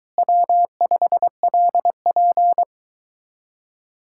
Morse code, heard as a keyed tone, about 23 wpm, 700 Hz. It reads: W5LP